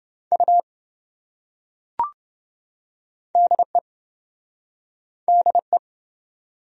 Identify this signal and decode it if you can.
Morse code, heard as a keyed tone, about 27 wpm, 700 Hz. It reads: U DE DE